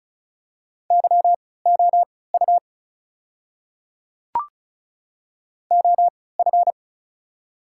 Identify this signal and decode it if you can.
Morse code, heard as a keyed tone, about 36 wpm, spaced as for 20 wpm, 700 Hz. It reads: YOU OF
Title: 3x Character Spacing - Top 100 Words - 35wpm